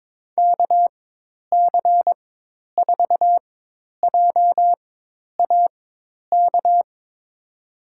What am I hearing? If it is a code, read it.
Morse code, heard as a keyed tone, about 21 wpm, spaced as for 10 wpm, 700 Hz. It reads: KC4JAK
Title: USA Call Signs - 6 Characters - 4x Character Spacing - 22wpm